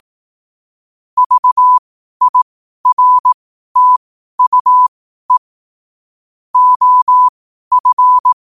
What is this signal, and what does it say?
Morse code, heard as a keyed tone, about 18 wpm, spaced as for 12 wpm, 1000 Hz. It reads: VIRTUE OF
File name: qotdfiles / 12wpm